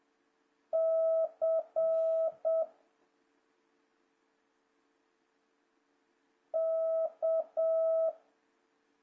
Morse code, heard as a keyed tone, about 7 wpm, 650 Hz. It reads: C K
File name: jh8cbh / 2026cw